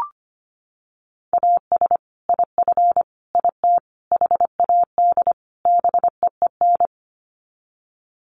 Morse code, heard as a keyed tone, about 25 wpm, 700 Hz. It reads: AS IF IT HAD BEEN
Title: Sets of 5 Words Encore2 25wpm